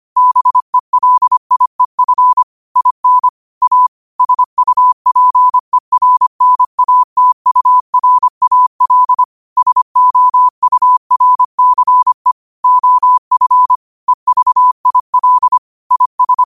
Morse code, heard as a keyed tone, about 25 wpm, 1000 Hz. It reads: BELIEF IN A SUPERNATURAL SOURCE OF EVIL IS
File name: qotdfiles / 25wpm